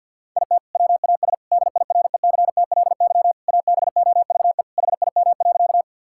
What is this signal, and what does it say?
Morse code, heard as a keyed tone, about 50 wpm, 700 Hz. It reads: IT WAS DIRECTLY ABOVE HIM.
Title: Sentences from Top 1000 Words - Review Part 5 - 50wpm